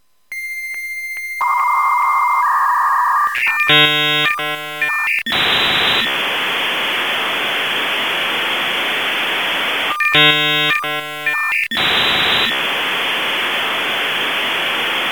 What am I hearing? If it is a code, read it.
Dial-up modem handshake.